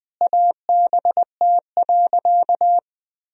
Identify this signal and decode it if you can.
Morse code, heard as a keyed tone, about 20 wpm, 700 Hz. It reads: ABT.
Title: Sentences from Top 400 Words and Abbreviations 20wpm